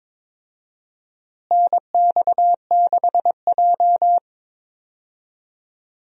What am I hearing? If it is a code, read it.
Morse code, heard as a keyed tone, about 22 wpm, 700 Hz. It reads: NX6J
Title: USA Call Signs - 4 Characters - Repeated 3x - 22wpm